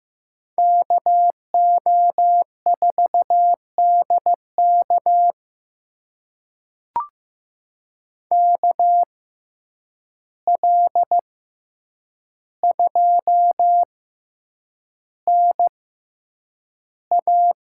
Morse code, heard as a keyed tone, about 15 wpm, spaced as for 8 wpm, 700 Hz. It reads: KO4DK KL2NA
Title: USA Call Signs - 5 Characters - 6x Character Spacing - 15wpm